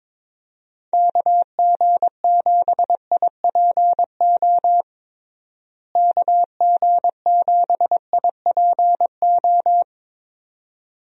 Morse code, heard as a keyed tone, about 22 wpm, 700 Hz. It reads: KG7IPO KG7IPO
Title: USA Call Signs - 6 Characters - Repeated 2x - 22wpm